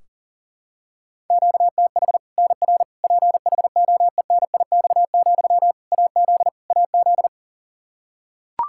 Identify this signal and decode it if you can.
Morse code, heard as a keyed tone, about 40 wpm, 700 Hz. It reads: QTH NR PHOENIX, AZ AZ
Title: K1USN SST QSO Practice 40wpm